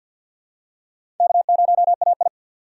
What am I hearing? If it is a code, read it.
Morse code, heard as a keyed tone, about 52 wpm, 700 Hz. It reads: K0AI
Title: USA Call Signs - 4 Characters - 6x Character Spacing - 50wpm